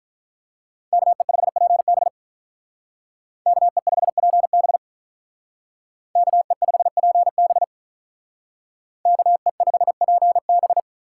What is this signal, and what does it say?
Morse code, heard as a keyed tone, about 45 wpm, 700 Hz. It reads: KE5PB KE5PB KE5PB KE5PB